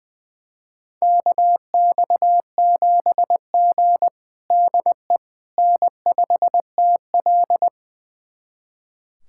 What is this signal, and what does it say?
Morse code, heard as a keyed tone, about 20 wpm, 700 Hz. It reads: KX7G DE N5TL
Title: USA Callsign DE Callsign - 4 Characters - 20wpm